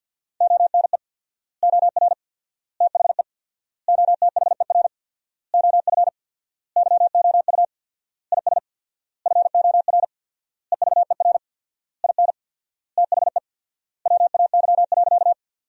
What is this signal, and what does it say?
Morse code, heard as a keyed tone, about 50 wpm, 700 Hz. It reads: ONE OR THE OTHER OF YOU IS FOR EVER IN THE WAY.
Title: 4x Word-Spacing - Sentences from Top 400 Words - 50wpm